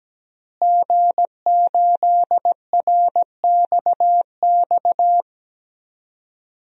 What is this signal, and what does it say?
Morse code, heard as a keyed tone, about 17 wpm, 700 Hz. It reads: G8RXX
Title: DXCC-100 Call Signs - 5 Characters - Repeated 2x - 17wpm